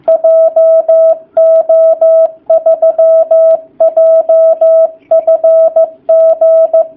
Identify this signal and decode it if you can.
Morse code, heard as a keyed tone, about 15 wpm, 660 Hz. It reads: JO3JFG
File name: jh8cbh / 2019cw / sp15- w